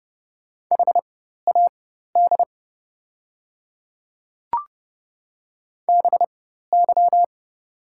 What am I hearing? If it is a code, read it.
Morse code, heard as a keyed tone, about 30 wpm, spaced as for 14 wpm, 700 Hz. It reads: HAD BY
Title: 4x Character Spacing - Top 100 Words - 30wpm